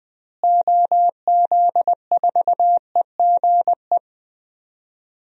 Morse code, heard as a keyed tone, about 21 wpm, 700 Hz. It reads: OZ4EGE
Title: DXCC-100 Call Signs - 6 Characters - Repeated 3x - 20wpm